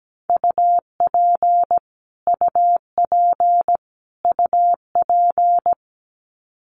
Morse code, heard as a keyed tone, about 17 wpm, 700 Hz. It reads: UP UP UP
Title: Top 200 Words Repeated 17wpm